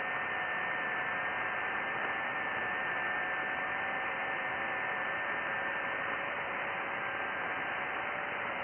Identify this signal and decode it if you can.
FT8.
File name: IC-7100